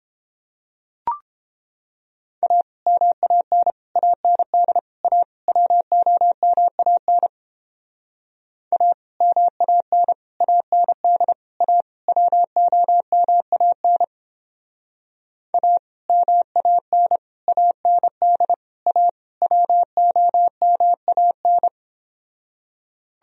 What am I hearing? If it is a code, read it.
Morse code, heard as a keyed tone, about 29 wpm, 700 Hz. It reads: A MAN AND A WOMAN A MAN AND A WOMAN A MAN AND A WOMAN